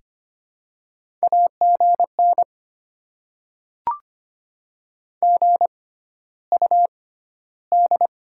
Morse code, heard as a keyed tone, about 25 wpm, spaced as for 13 wpm, 700 Hz. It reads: AGN GUD
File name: Top 20 QSO Elements - 6x Character Spacing - 25wpm